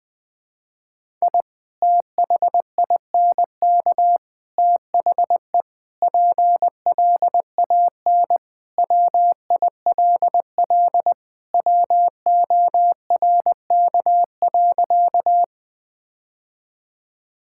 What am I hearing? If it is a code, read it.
Morse code, heard as a keyed tone, about 20 wpm, 700 Hz. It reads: I THINK THE PLAN WILL WORK.
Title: Sentences with 3-Word Phrases - Spoken Beginning - 20wpm